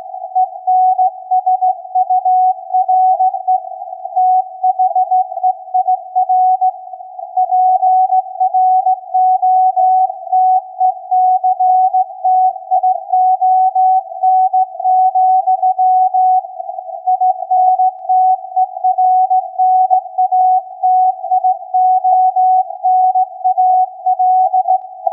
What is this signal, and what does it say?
Morse code, heard as a keyed tone, about 16 wpm, 740 Hz. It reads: ENSURE THEIR PROTECTION, INTERNATIONAL
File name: iz7slz / VLF / SAQ 16-11-2022